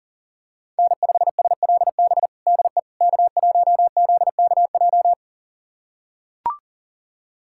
Morse code, heard as a keyed tone, about 40 wpm, 700 Hz. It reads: N5SLB DE K1ZKJ
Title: USA Callsign DE Callsign 40wpm